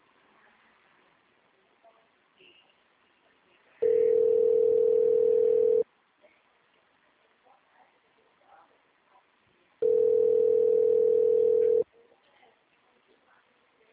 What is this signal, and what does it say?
Ringback tone (North America), 440 + 480 Hz, with speech.
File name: out- -2021- - - .72252